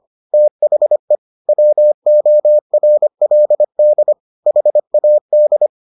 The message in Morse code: THE WORLD HAD

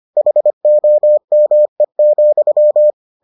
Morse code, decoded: HOME,

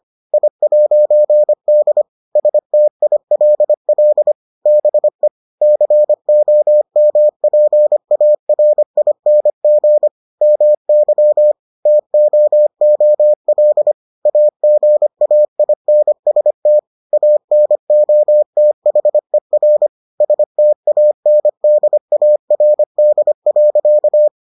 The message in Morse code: I'D STILL BE COMPARING MY TOOL AGAINST ANOTHER STANDARD.

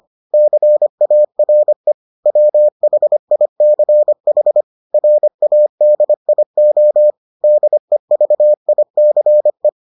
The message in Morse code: CARE WHICH RADIO DEVICE

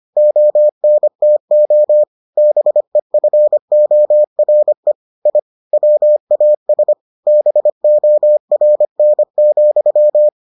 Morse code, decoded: ONTO BEFORE I WAS BORN,

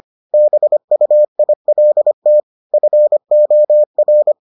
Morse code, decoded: BUILT FOR

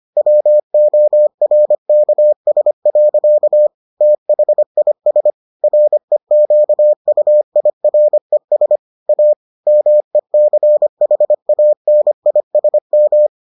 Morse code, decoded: WORKS. THIS REQUIRES A MECHANISM

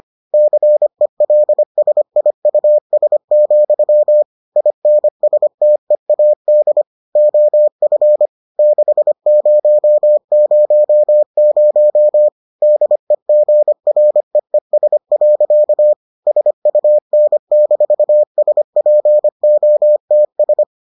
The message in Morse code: CELSIUS, INSTEAD OF 6000 DEGREES. SUN-SPOTS

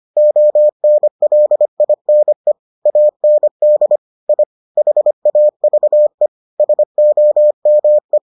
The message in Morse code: ONLINE AND I HAVE SOME

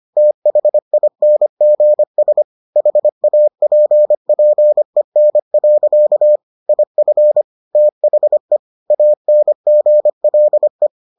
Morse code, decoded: THINGS HAPPEN. IF THE ANGLE

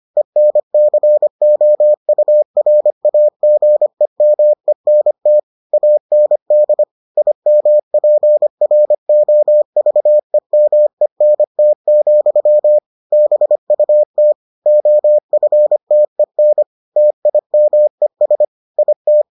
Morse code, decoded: ENCOURAGEMENT AND IMPROVEMENT, BUT OFTEN TIMES IT

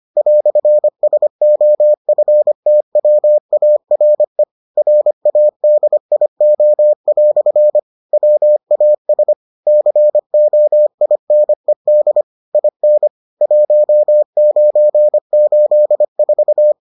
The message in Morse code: "SOFTWARE RADIO" WAS COINED IN 1984

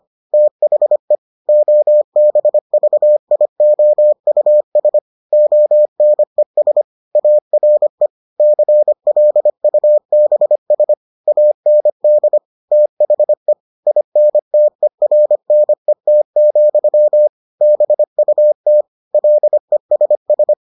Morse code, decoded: THE OBVIOUS ONES ARE CLUBS AND THE INTERNET, BUT LESS

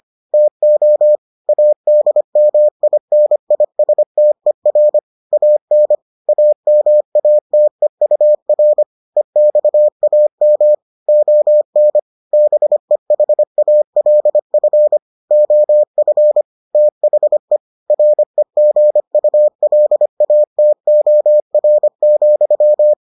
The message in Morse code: TO ADMINISTER AN AMATEUR EXAM ON BEHALF OF THE REGULATOR,

